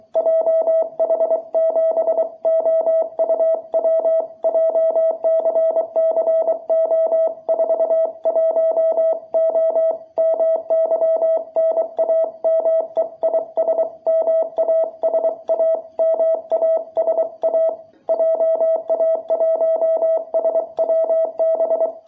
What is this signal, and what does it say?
Morse code, heard as a keyed tone, about 23 wpm, 660 Hz. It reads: JH7OUW JCCO41O MY NAME IS MASA MASA JA1SWB
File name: jh8cbh / 2021cw